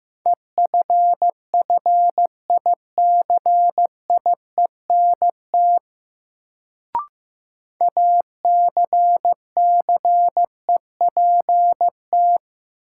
Morse code, heard as a keyed tone, about 15 wpm, 700 Hz. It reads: EFFICIENT ACCEPT
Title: Sentences from Top 1300 Words - New Word Drill Alt - 15wpm